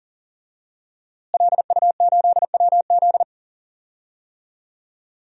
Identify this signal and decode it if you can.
Morse code, heard as a keyed tone, about 40 wpm, 700 Hz. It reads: LU8WZ
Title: DXCC-100 Call Signs - 5 Characters - Repeated 3x - 40wpm